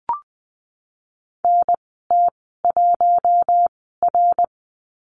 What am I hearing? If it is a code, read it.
Morse code, heard as a keyed tone, about 20 wpm, spaced as for 14 wpm, 700 Hz. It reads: NT1R
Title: 055 Lesson 10g - Letter r - Call Signs 20wpm